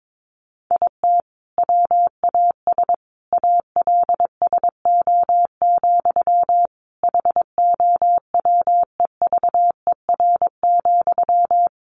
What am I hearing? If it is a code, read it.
Morse code, heard as a keyed tone, about 22 wpm, 700 Hz. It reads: IT WAS ALSO, HOWEVER,
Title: Sentences from Top 1400 Words 22wpm